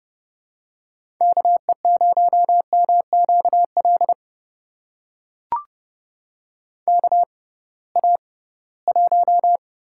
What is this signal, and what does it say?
Morse code, heard as a keyed tone, about 30 wpm, spaced as for 19 wpm, 700 Hz. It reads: KE0MQL KA1
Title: USA Call Signs - 6 Characters - 6x Character Spacing - 30wpm